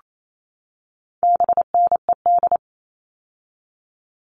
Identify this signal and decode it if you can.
Morse code, heard as a keyed tone, about 28 wpm, 700 Hz. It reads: BNED
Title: All Cut Numbers - 4-Digit 28wpm